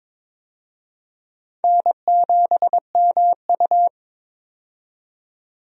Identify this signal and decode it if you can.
Morse code, heard as a keyed tone, about 22 wpm, 700 Hz. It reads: N7MU
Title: USA Call Signs - 4 Characters - 22wpm